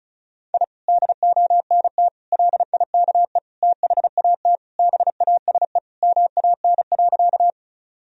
Morse code, heard as a keyed tone, about 36 wpm, 700 Hz. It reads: I DONT LIKE THAT BASE MAN.
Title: Sentences from Top 1000 Words - Review Part 1 - 35wpm